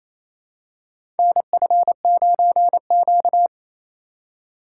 Morse code, heard as a keyed tone, about 27 wpm, 700 Hz. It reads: NF9Q
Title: USA Call Signs - 4 Characters - 2x Character Spacing - 28wpm